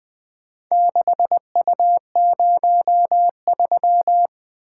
Morse code, heard as a keyed tone, about 20 wpm, 700 Hz. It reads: 6U03
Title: Four Letter-Number Combinations - 20wpm